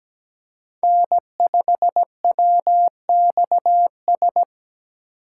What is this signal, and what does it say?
Morse code, heard as a keyed tone, about 17 wpm, 700 Hz. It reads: N5WXS
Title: USA Call Signs - 5 Characters - 4x Character Spacing - 17wpm